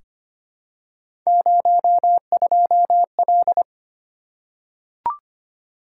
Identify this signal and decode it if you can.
Morse code, heard as a keyed tone, about 25 wpm, 700 Hz. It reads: 02L
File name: Three Letter-Number Combinations - 25wpm